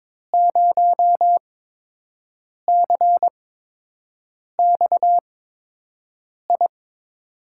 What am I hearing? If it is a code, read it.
Morse code, heard as a keyed tone, about 22 wpm, 700 Hz. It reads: 0 C X I
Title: USA Call Signs - 6 Characters - 8x Character Spacing - 22wpm